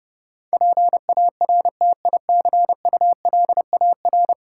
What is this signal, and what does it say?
Morse code, heard as a keyed tone, about 30 wpm, 700 Hz. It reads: PARTICULAR
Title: Sentences from Top 600 Words - New Word Drill - 30wpm